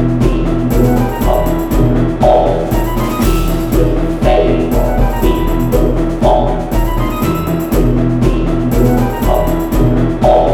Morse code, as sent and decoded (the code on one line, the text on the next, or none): none